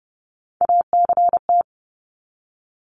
Morse code, heard as a keyed tone, about 30 wpm, 700 Hz. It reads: ACT
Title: Top 300 Words 30wpm